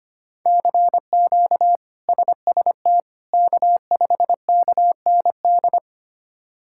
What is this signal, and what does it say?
Morse code, heard as a keyed tone, about 25 wpm, 700 Hz. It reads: CQ SST K5KND
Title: K1USN SST QSO Practice - 25wpm